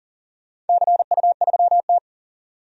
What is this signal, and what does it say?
Morse code, heard as a keyed tone, about 40 wpm, 700 Hz. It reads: CU3T
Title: DXCC-100 Call Signs - 4 Characters - 40wpm